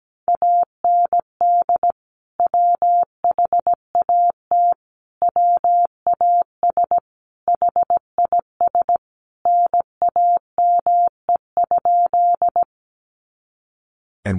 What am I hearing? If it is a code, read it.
Morse code, heard as a keyed tone, about 17 wpm, 700 Hz. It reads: AND WHAT WAS HIS NAME?
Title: Sentences from Top 200 Words 17wpm